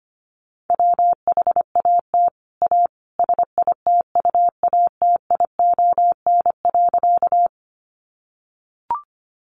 Morse code, heard as a keyed tone, about 25 wpm, 700 Hz. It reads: WHAT A SITUATION.
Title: Sentences from Top 1200 Words 25wpm